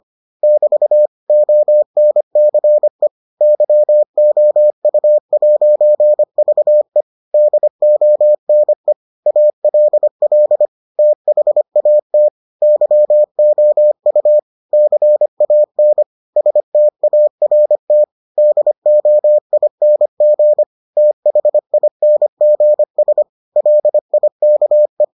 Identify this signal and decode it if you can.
Morse code, heard as a keyed tone, about 25 wpm, 600 Hz. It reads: <BT> ONCE YOU'VE DONE ALL THAT YOU CAN START DOING THINGS LIKE